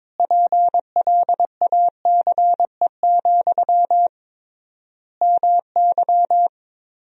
Morse code, heard as a keyed tone, about 21 wpm, 700 Hz. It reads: PLACE, MY
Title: 3x Word-Spacing - Sentences from Top 400 Words - 22wpm